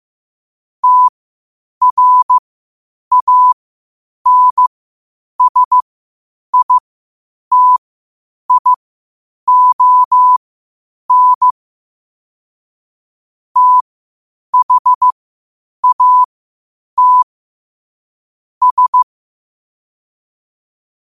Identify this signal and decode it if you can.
Morse code, heard as a keyed tone, about 15 wpm, spaced as for 8 wpm, 1000 Hz. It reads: TRANSITION THAT S
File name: qotdfiles / 8wpm